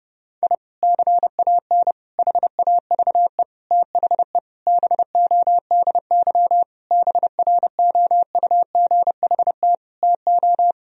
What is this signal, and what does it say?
Morse code, heard as a keyed tone, about 30 wpm, 700 Hz. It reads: I CAN HAVE THE BODY BROUGHT TO
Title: Sentences from Top 1000 Words - Review Part 2 - 30wpm